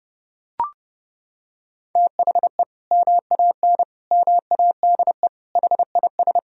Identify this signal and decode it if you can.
Morse code, heard as a keyed tone, about 32 wpm, 700 Hz. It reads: THE MAN MADE HIS
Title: Sentences from Top 1000 Words - Review Part 2 - 30wpm